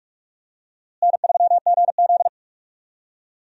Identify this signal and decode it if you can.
Morse code, heard as a keyed tone, about 46 wpm, 700 Hz. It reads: N3GZ
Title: USA Call Signs - 4 Characters - Repeated 2x - 45wpm